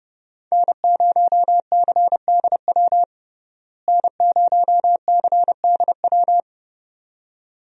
Morse code, heard as a keyed tone, about 30 wpm, 700 Hz. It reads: N0CDW N0CDW
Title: USA Call Signs - 5 Characters - Repeated 2x - 30wpm